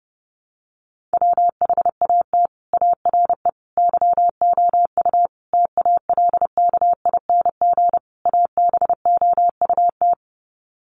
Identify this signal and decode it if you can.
Morse code, heard as a keyed tone, about 30 wpm, 700 Hz. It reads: WHAT ARE YOU TALKING ABOUT
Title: Sets of 5 Words 30wpm